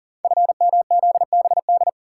Morse code, heard as a keyed tone, about 39 wpm, 700 Hz. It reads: FMZBD
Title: Code Groups - 5 Letters - 40wpm